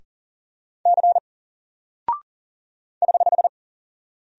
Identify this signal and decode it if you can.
Morse code, heard as a keyed tone, about 38 wpm, 700 Hz. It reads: C <HH>
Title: Single Character (Letter:Number:Punctuation:Prosign) 40wpm